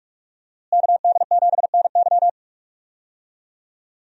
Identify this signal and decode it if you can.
Morse code, heard as a keyed tone, about 46 wpm, 700 Hz. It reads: KD7NY
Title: USA Call Signs - 5 Characters - Repeated 3x - 45wpm